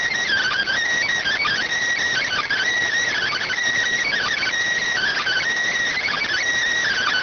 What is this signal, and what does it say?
Emergency Alert System (SAME) tones (header bursts), heard as harsh buzzing bursts. It is heard from 1.1 to 6.6 s.